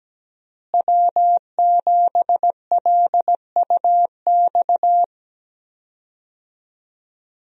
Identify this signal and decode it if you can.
Morse code, heard as a keyed tone, about 17 wpm, 700 Hz. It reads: W7LUX
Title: USA Call Signs - 5 Characters - Repeated 3x - 17wpm